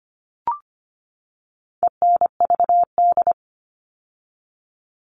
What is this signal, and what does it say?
Morse code, heard as a keyed tone, about 25 wpm, 700 Hz. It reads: ENVD
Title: All Cut Numbers - 4-Digit 25wpm